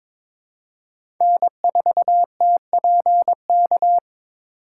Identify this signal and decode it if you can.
Morse code, heard as a keyed tone, about 22 wpm, 700 Hz. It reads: N4TPK
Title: USA Call Signs - 3x Character Spacing - 22wpm